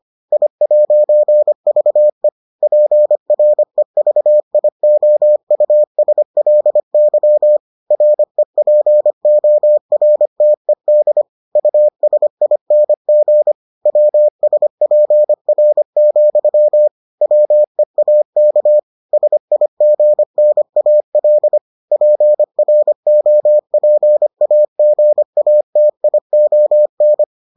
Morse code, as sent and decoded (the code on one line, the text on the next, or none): .. .----. ...- . / .--. .-. ...- .. --- ..- ... .-.. -.-- / .-. .--. --- .-. - . -.. / ..- ... .. -. --. / .-- ... .--. .-. --..-- / .-- . .- -.- / ... .. --. -. .- .-.. / .--. .-. --- .--. .- --. .- - .. --- -.
I'VE PREVIOUSLY REPORTED USING WSPR, WEAK SIGNAL PROPAGATION